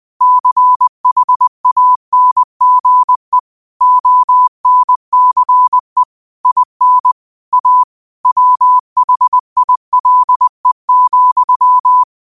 Morse code, heard as a keyed tone, about 20 wpm, 1000 Hz. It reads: CHANGE ONCE IN A WHILE,
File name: qotdfiles / 20wpm